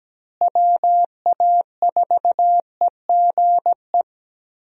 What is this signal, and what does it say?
Morse code, heard as a keyed tone, about 17 wpm, 700 Hz. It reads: WA4EGE
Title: USA Call Signs - 6 Characters - Repeated 3x - 17wpm